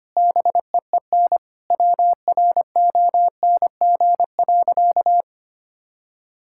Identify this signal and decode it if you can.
Morse code, heard as a keyed tone, about 25 wpm, 700 Hz. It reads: BEEN WRONG.
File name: practice / 25wpm / Sentences with 9-Word Phrases - 25wpm